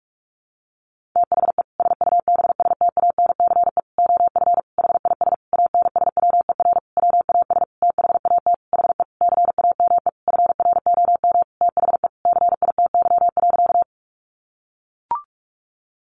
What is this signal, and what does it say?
Morse code, heard as a keyed tone, about 44 wpm, 700 Hz. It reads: THE SUBSTANCE OF HIS ANSWER WAS THAT HE CAME FROM THE CITY.
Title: Sentences from Top 1500 Words 45wpm